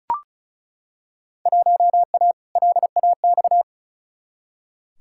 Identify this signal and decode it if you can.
Morse code, heard as a keyed tone, about 36 wpm, 700 Hz. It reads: 1A LAX
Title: Field Day Exchanges - 35wpm